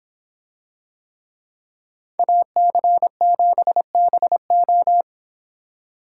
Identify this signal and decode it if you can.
Morse code, heard as a keyed tone, about 26 wpm, 700 Hz. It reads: AC7BO